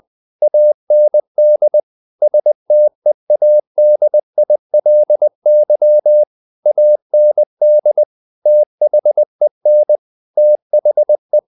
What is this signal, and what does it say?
Morse code, heard as a keyed tone, about 20 wpm, 600 Hz. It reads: AND STEADILY AND THEN THE
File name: the old man and the sea / the old man and the sea part 1 20wpm